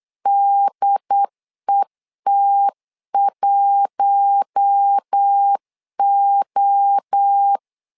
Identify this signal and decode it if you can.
Morse code, heard as a keyed tone, about 8 wpm, 800 Hz. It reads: DET1O
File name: LEZIONI / 4 Q1S 8wpm 2710